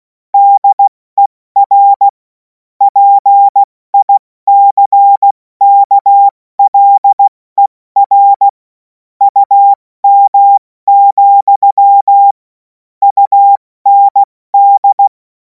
Morse code, heard as a keyed tone, about 16 wpm, 800 Hz. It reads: DER PICKLER UM, UND
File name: Morsekurs / Witze / Witz 17. Februar 2026 70 F80